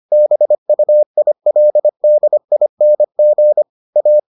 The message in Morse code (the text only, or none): BUILDING A